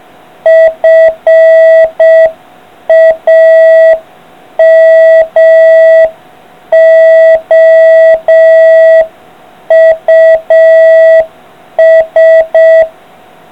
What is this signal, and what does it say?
Morse code, heard as a keyed tone, about 6 wpm, 650 Hz. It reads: FAMOUS